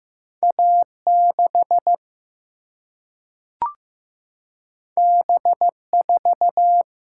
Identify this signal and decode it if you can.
Morse code, heard as a keyed tone, about 15 wpm, 700 Hz. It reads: A 6 B 4
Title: Single-Letter Single-Number 15wpm